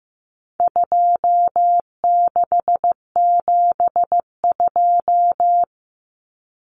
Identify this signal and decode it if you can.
Morse code, heard as a keyed tone, about 15 wpm, 700 Hz. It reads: 2672
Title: Numbers - 4-Digit 15wpm